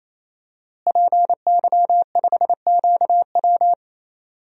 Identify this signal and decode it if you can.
Morse code, heard as a keyed tone, about 27 wpm, 700 Hz. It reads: PY5QW
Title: DXCC-100 Call Signs - 5 Characters - Repeated 3x - 28wpm